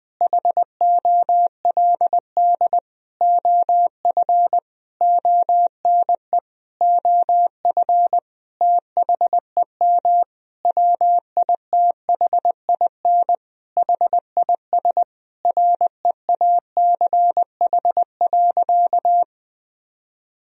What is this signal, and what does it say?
Morse code, heard as a keyed tone, about 20 wpm, 700 Hz. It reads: HOLD OF ONE OF THEM WITHIN HIS REACH.
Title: Sentences from Top 1000 Words - Review Part 4 - 20wpm